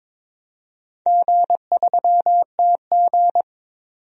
Morse code, heard as a keyed tone, about 23 wpm, 690 Hz. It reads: G3TG
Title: DXCC-100 Call Signs - 4 Characters - 22wpm